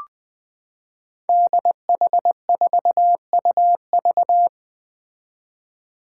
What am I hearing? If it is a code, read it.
Morse code, heard as a keyed tone, about 20 wpm, 700 Hz. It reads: DH4UV
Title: Five Challenging Letter-Number Combinations 20wpm